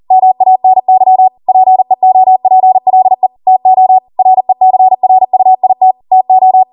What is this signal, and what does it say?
Morse code, heard as a keyed tone, about 40 wpm, 750 Hz. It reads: MANY PEOPLE TO RECRUIT TO